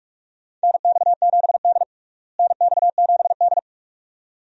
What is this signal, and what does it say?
Morse code, heard as a keyed tone, about 46 wpm, 700 Hz. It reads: NX7D NX7D
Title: USA Call Signs - 4 Characters - Repeated 2x - 45wpm